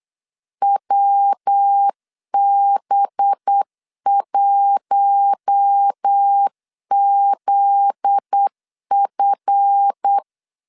Morse code, heard as a keyed tone, about 8 wpm, 800 Hz. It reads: WB1ZF
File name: LEZIONI / 7 GF9Z / 7 GF9Z 8wpm 3213